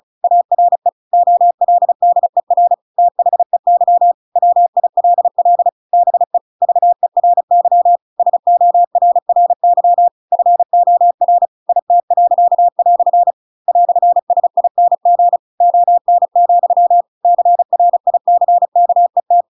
Morse code, heard as a keyed tone, about 36 wpm, 700 Hz. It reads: ARE OLDER THEY WILL BE VERY SORRY FOR IT." "SING ON, CRICKET